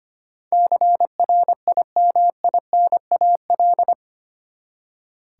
Morse code, heard as a keyed tone, about 25 wpm, 700 Hz. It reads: CRIMINAL
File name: Eight Letter Words - 25wpm